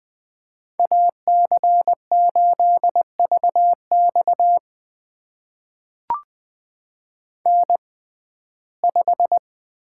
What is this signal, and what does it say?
Morse code, heard as a keyed tone, about 21 wpm, spaced as for 15 wpm, 700 Hz. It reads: AC8VX N5
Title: USA Call Signs - 5 Characters - 6x Character Spacing - 20wpm